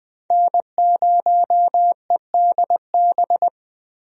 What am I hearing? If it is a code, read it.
Morse code, heard as a keyed tone, about 20 wpm, 700 Hz. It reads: N0EDB
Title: USA Call Signs - 5 Characters - 20wpm